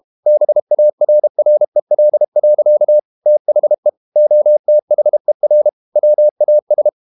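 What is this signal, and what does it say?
Morse code, heard as a keyed tone, about 32 wpm, 600 Hz. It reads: BARREL. THE OTHER WAS
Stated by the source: Morse code practice at 32 wpm from CW med Gnister, reading Ken Follett's novel Never in English.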